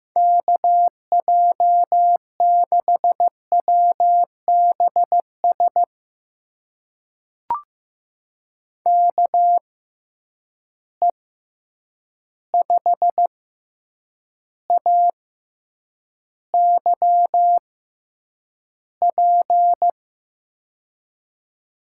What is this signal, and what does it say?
Morse code, heard as a keyed tone, about 15 wpm, spaced as for 8 wpm, 700 Hz. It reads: KJ6WBS KE5AYP
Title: USA Call Signs - 6 Characters - 6x Character Spacing - 15wpm